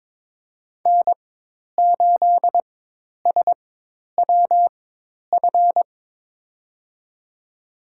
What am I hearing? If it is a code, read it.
Morse code, heard as a keyed tone, about 22 wpm, spaced as for 10 wpm, 700 Hz. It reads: N8SWF
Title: USA Call Signs - 5 Characters - 4x Character Spacing - 22wpm